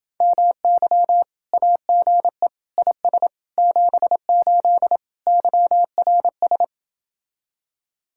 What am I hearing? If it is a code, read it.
Morse code, heard as a keyed tone, about 29 wpm, 700 Hz. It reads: MY AGE IS 78 YRS